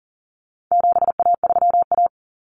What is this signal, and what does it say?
Morse code, heard as a keyed tone, about 40 wpm, 700 Hz. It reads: 7A3A
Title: Common Cut Numbers - 4-Digit 40wpm